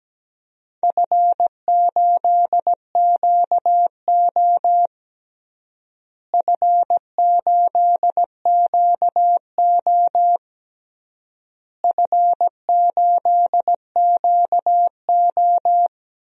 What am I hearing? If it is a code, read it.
Morse code, heard as a keyed tone, about 17 wpm, 700 Hz. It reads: F8QO F8QO F8QO
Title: DXCC-100 Call Signs - 4 Characters - Repeated 3x - 17wpm